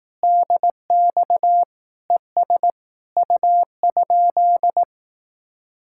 Morse code, heard as a keyed tone, about 18 wpm, 700 Hz. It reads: DX ES U?
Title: Speed-Racing Common QSO Phrases Encore - 15wpm